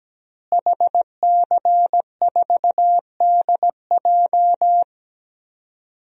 Morse code, heard as a keyed tone, about 17 wpm, 700 Hz. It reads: HC4DJ
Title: DXCC-100 Call Signs - 5 Characters - Repeated 3x - 17wpm